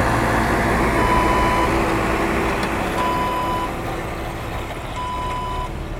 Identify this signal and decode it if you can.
There is no code.